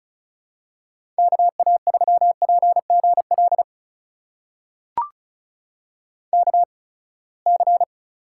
Morse code, heard as a keyed tone, about 36 wpm, spaced as for 24 wpm, 700 Hz. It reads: KA3PGL KC
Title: USA Call Signs - 6 Characters - 8x Character Spacing - 35wpm